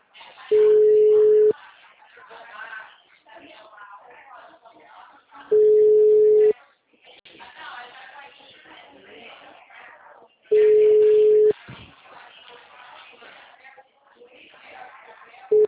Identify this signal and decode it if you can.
Ringback tone (Europe, 425 Hz), with speech.